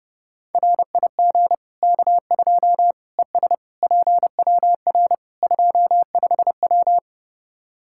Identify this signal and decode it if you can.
Morse code, heard as a keyed tone, about 30 wpm, 700 Hz. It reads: RIG K2 ES PWR 25W
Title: Practice QSO - Exchange 2 - Info - 30wpm